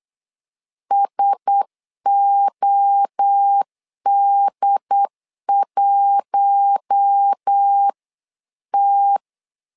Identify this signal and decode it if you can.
Morse code, heard as a keyed tone, about 8 wpm, 800 Hz. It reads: SOD1T
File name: LEZIONI / 4 Q1S 8wpm 2710